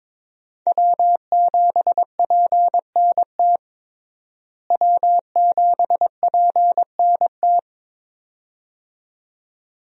Morse code, heard as a keyed tone, about 22 wpm, 700 Hz. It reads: W7PNT W7PNT
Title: USA Call Signs - 5 Characters - Repeated 2x - 22wpm